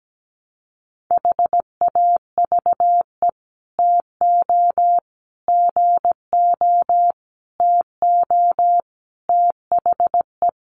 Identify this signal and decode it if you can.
Morse code, heard as a keyed tone, about 17 wpm, 700 Hz. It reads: HAVE TO GO TO THE
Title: Sets of 5 Words Encore 17wpm